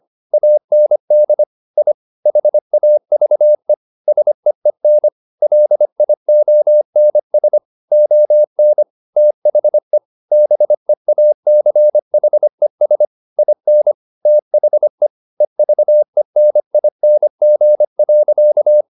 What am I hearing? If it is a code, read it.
Morse code, heard as a keyed tone, about 25 wpm, 600 Hz. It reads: AND I HAVE SEEN LIONS ON THE BEACHES IN THE EVENING.
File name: the old man and the sea part 1 25wpm